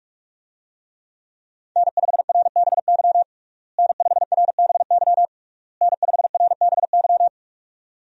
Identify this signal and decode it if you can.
Morse code, heard as a keyed tone, about 41 wpm, 700 Hz. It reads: N5RBY N5RBY N5RBY
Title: USA Call Signs - 5 Characters - Repeated 3x - 45wpm